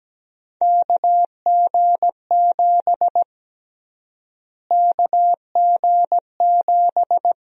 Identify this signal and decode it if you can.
Morse code, heard as a keyed tone, about 17 wpm, 700 Hz. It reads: KG7 KG7
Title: USA Call Sign Prefixes - Repeated 2x - 17wpm